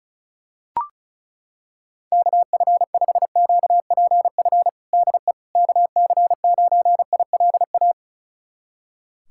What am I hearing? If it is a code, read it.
Morse code, heard as a keyed tone, about 36 wpm, 700 Hz. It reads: KF5QPF DE KC9ILA
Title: USA Callsign DE Callsign 35wpm